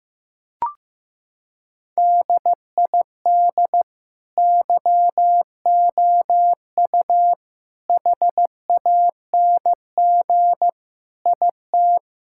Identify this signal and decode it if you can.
Morse code, heard as a keyed tone, about 15 wpm, 700 Hz. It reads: DID YOU HANG IT
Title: Sentences from Top 1600 Words 15wpm